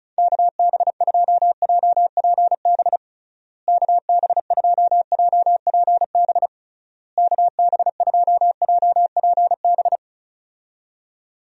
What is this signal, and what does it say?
Morse code, heard as a keyed tone, about 36 wpm, 700 Hz. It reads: KB2JPB KB2JPB KB2JPB
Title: USA Call Signs - 6 Characters - Repeated 3x - 35wpm